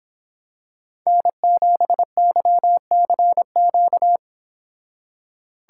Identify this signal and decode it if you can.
Morse code, heard as a keyed tone, about 26 wpm, 700 Hz. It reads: N7YCQ